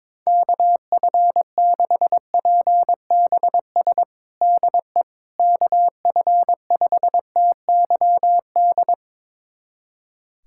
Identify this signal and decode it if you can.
Morse code, heard as a keyed tone, about 22 wpm, 700 Hz. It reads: KF6PBS DE KF5TYD
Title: USA Callsign DE Callsign - 6 Characters - 22wpm